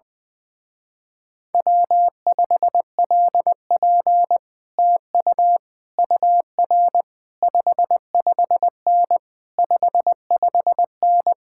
Morse code, heard as a keyed tone, about 20 wpm, 700 Hz. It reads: W5LP TU UR 55N 55N
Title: Speed-Racing Common QSO Phrases - 20wpm